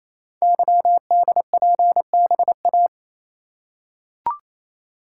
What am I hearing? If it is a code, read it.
Morse code, heard as a keyed tone, about 28 wpm, 700 Hz. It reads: YDPBA
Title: Code Groups - 5 Letters - 28wpm